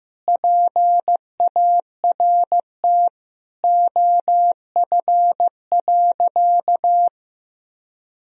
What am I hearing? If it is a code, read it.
Morse code, heard as a keyed tone, about 15 wpm, 700 Hz. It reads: PART OF.
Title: Sentences from Top 1000 Words - Review Part 1 - 15wpm